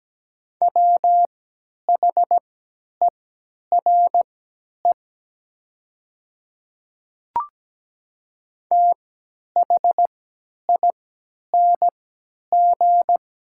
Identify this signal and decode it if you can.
Morse code, heard as a keyed tone, about 17 wpm, spaced as for 10 wpm, 700 Hz. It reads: WHERE THING
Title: 3x Character Spacing - Top 200 Words - 17wpm